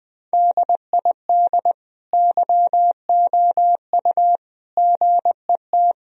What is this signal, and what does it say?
Morse code, heard as a keyed tone, about 20 wpm, 700 Hz. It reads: DID YOU GET
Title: Sentences from Top 1000 Words - Review Part 3 - 20wpm